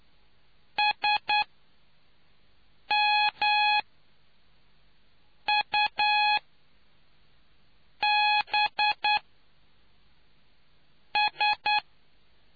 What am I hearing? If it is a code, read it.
Morse code, heard as a keyed tone, about 10 wpm, spaced as for 4 wpm, 840 Hz. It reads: SMUBS